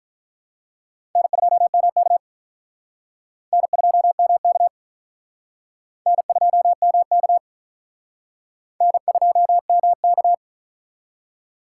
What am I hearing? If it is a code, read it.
Morse code, heard as a keyed tone, about 44 wpm, 700 Hz. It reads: N2MK N2MK N2MK N2MK